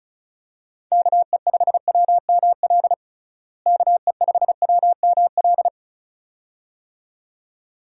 Morse code, heard as a keyed tone, about 36 wpm, 700 Hz. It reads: KE5WML KE5WML
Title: USA Call Signs - 6 Characters - Repeated 2x - 35wpm